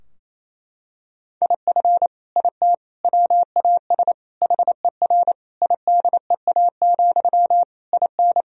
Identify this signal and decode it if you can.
Morse code, heard as a keyed tone, about 28 wpm, 700 Hz. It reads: IF IT WAS HER IDEA, IN
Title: Sentences from Top 1000 Words - Review Part 2 - 28wpm